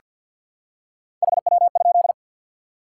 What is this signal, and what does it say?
Morse code, heard as a keyed tone, about 47 wpm, 700 Hz. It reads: HW?
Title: Top 90 QSO Elements 50wpm